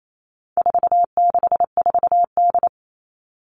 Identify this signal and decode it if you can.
Morse code, heard as a keyed tone, about 28 wpm, 700 Hz. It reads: 464D
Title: All Cut Numbers - 4-Digit 28wpm